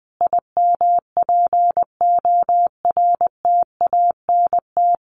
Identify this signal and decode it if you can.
Morse code, heard as a keyed tone, about 20 wpm, 700 Hz. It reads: IMPORTANT
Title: Sentences from Top 1400 Words 20wpm